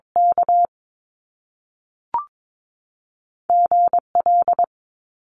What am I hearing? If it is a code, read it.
Morse code, heard as a keyed tone, about 22 wpm, 700 Hz. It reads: K GL
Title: Top Two Letter Combinations in Words 22wpm